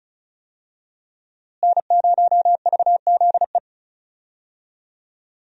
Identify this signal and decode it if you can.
Morse code, heard as a keyed tone, about 34 wpm, 700 Hz. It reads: N0VZE
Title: USA Call Signs - 5 Characters - Repeated 3x - 35wpm